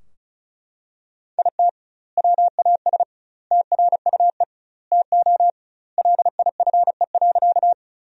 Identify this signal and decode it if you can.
Morse code, heard as a keyed tone, about 34 wpm, 700 Hz. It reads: IT WAS TRUE TO LIFE.
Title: 2x Word-Spacing - Sentences from Top 400 Words - 35wpm